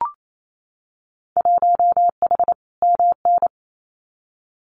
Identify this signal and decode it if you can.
Morse code, heard as a keyed tone, about 28 wpm, 700 Hz. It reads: 1H MN
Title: Winter Field Day Exchanges 28wpm